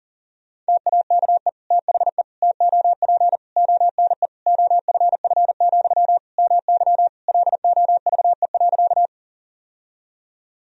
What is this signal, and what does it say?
Morse code, heard as a keyed tone, about 40 wpm, 700 Hz. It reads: TAKE THE TOP ONE OFF, MY LOVE.
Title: Sentences from Top 1000 Words - Review Part 2 - 40wpm